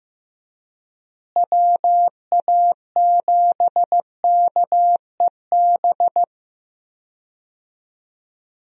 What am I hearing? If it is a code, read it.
Morse code, heard as a keyed tone, about 15 wpm, 700 Hz. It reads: WA7KEB
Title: USA Call Signs - 6 Characters - Repeated 2x - 15wpm